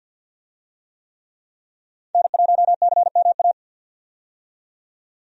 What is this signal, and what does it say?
Morse code, heard as a keyed tone, about 45 wpm, 700 Hz. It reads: N1CMA
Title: USA Call Signs - 5 Characters - 50wpm